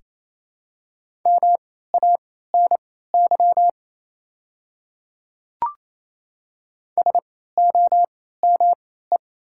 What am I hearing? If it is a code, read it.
Morse code, heard as a keyed tone, about 28 wpm, spaced as for 16 wpm, 700 Hz. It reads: MANY SOME
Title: 3x Character Spacing - Top 300 Words - 28wpm